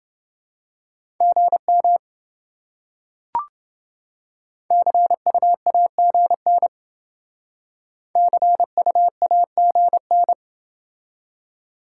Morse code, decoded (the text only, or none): GM CUAGN CUAGN